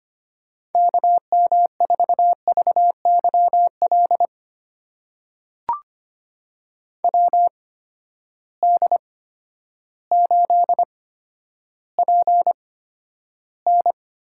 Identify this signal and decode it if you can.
Morse code, heard as a keyed tone, about 25 wpm, spaced as for 11 wpm, 700 Hz. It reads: KM4VYL WD8PN
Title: USA Call Signs - 6 Characters - 8x Character Spacing - 25wpm